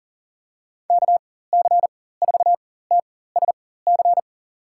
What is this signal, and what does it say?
Morse code, heard as a keyed tone, about 37 wpm, spaced as for 18 wpm, 700 Hz. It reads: KC4TSC
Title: USA Call Signs - 6 Characters - 4x Character Spacing - 40wpm